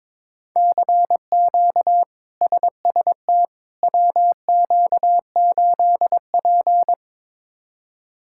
Morse code, heard as a keyed tone, about 22 wpm, 700 Hz. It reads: CQ SST WQ8P
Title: K1USN SST QSO Practice - 22wpm